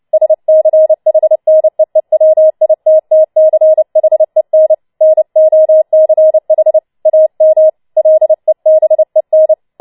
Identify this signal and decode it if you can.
Morse code, heard as a keyed tone, about 29 wpm, 630 Hz. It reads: SCHNEEWITTCHEN NOCH AM LEBEN
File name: Schneewittchen Teil3 30wpm